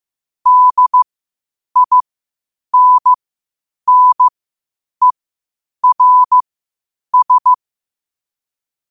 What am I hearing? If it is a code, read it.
Morse code, heard as a keyed tone, about 15 wpm, spaced as for 8 wpm, 1000 Hz. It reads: DINNERS
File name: 8wpm